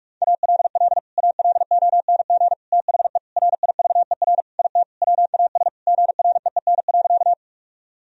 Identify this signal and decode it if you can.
Morse code, heard as a keyed tone, about 41 wpm, 700 Hz. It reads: ALL ALONG THE RIVER IT WAS GREEN.
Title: Sentences from Top 1000 Words - Review Part 2 - 45wpm